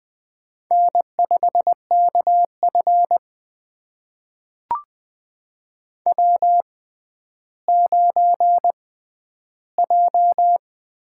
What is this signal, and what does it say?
Morse code, heard as a keyed tone, about 20 wpm, spaced as for 11 wpm, 700 Hz. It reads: N5KF W9J
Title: USA Call Signs - 4 Characters - 6x Character Spacing - 20wpm